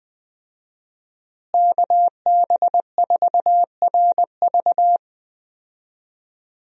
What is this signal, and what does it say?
Morse code, heard as a keyed tone, about 21 wpm, 700 Hz. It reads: KB4RV
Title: USA Call Signs - 5 Characters - Repeated 3x - 20wpm